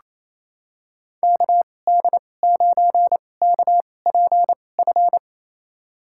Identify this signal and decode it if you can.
Morse code, heard as a keyed tone, about 28 wpm, spaced as for 20 wpm, 700 Hz. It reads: KD9KPF
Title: USA Call Signs - 6 Characters - 2x Character Spacing - 28wpm